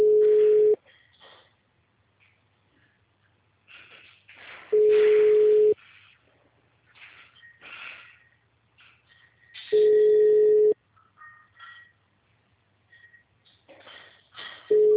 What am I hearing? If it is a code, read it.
Ringback tone (Europe, 425 Hz).